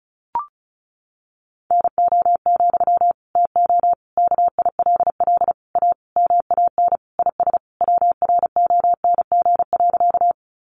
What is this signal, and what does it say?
Morse code, heard as a keyed tone, about 34 wpm, 700 Hz. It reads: NO, TO KILL A MAN IS WRONG.
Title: Sentences from Top 1400 Words 35wpm